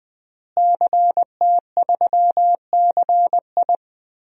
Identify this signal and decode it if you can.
Morse code, heard as a keyed tone, about 21 wpm, 700 Hz. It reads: CT3CI
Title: DXCC-100 Call Signs - 5 Characters - Repeated 3x - 20wpm